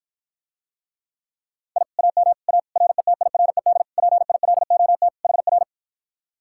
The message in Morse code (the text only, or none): I AM A RETIRED PILOT HR